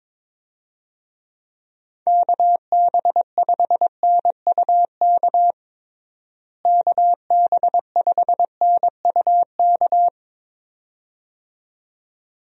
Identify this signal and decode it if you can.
Morse code, heard as a keyed tone, about 21 wpm, 700 Hz. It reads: KB5NUK KB5NUK
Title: USA Call Signs - 6 Characters - Repeated 2x - 22wpm